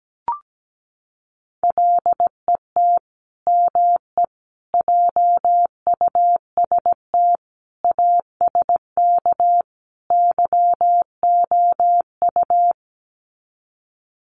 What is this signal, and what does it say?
Morse code, heard as a keyed tone, about 17 wpm, 700 Hz. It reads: LET ME JUST ASK YOU
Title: Sets of 5 Words Encore2 17wpm